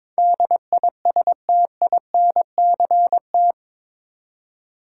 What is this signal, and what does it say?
Morse code, heard as a keyed tone, about 21 wpm, 700 Hz. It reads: DISTINCT
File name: practice / 22wpm / Eight Letter Words - 22wpm